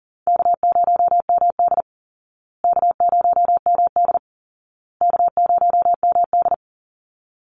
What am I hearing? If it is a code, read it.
Morse code, heard as a keyed tone, about 40 wpm, 700 Hz. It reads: K0MD K0MD K0MD
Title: Easy Call Signs Repeated 3x - 40wpm